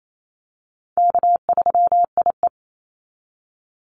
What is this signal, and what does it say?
Morse code, heard as a keyed tone, about 28 wpm, 700 Hz. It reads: K3IE
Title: Call Signs - 28wpm